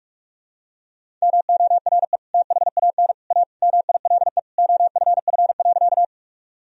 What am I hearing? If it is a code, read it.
Morse code, heard as a keyed tone, about 46 wpm, 700 Hz. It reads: MORE THAN A MILE OFF.
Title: Sentences from Top 1000 Words - Review Part 1 - 45wpm